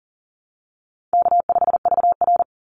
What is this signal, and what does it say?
Morse code, heard as a keyed tone, about 40 wpm, 700 Hz. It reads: K5VR
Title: Call Signs Repeated 3x - 40wpm